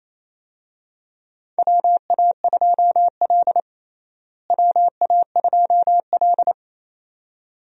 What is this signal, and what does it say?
Morse code, heard as a keyed tone, about 28 wpm, 700 Hz. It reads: WA2L WA2L
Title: USA Call Signs - 4 Characters - Repeated 2x - 28wpm